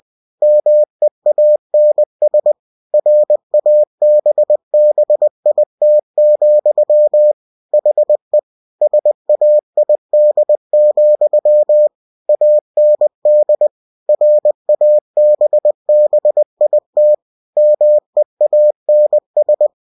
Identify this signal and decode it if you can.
Morse code, heard as a keyed tone, about 20 wpm, 600 Hz. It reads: MEANS RABBIT, HE SAID, AND RABBIT MEANS
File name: cw-books / winnie-the-pooh / winnie-20wpm-page-0013